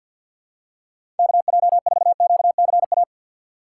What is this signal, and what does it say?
Morse code, heard as a keyed tone, about 49 wpm, 700 Hz. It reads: KJ4QCA
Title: USA Call Signs - 6 Characters - 6x Character Spacing - 50wpm